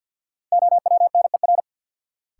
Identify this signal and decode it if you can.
Morse code, heard as a keyed tone, about 53 wpm, 700 Hz. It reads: OWNER
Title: Sentences from Top 1600 Words - New Word Drill - 50wpm